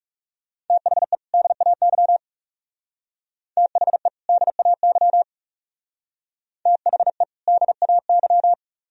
Morse code, heard as a keyed tone, about 38 wpm, 700 Hz. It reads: THE DAY THE DAY THE DAY